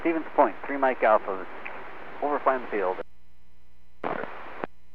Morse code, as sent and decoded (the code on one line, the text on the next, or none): none